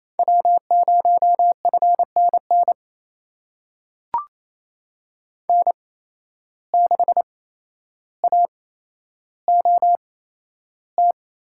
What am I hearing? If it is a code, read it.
Morse code, heard as a keyed tone, about 28 wpm, spaced as for 12 wpm, 700 Hz. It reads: W0FNN N6AOT
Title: USA Call Signs - 5 Characters - 8x Character Spacing - 28wpm